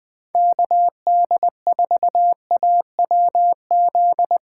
Morse code, heard as a keyed tone, about 21 wpm, 700 Hz. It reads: KD4AWZ
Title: Speed Racing - USA 2x3 Call Signs 17wpm